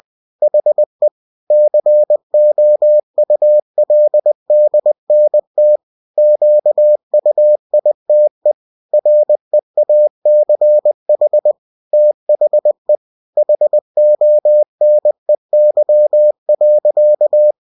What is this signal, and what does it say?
Morse code, heard as a keyed tone, about 20 wpm, 600 Hz. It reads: HE COULDNT QUITE REACH THE HONEY.
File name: cw-books / winnie-the-pooh / winnie-20wpm-page-0007